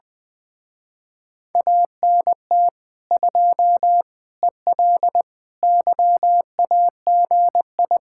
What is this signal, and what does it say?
Morse code, heard as a keyed tone, about 20 wpm, 700 Hz. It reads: ANT 2 EL YAGI
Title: Speed-Racing Common QSO Phrases - 20wpm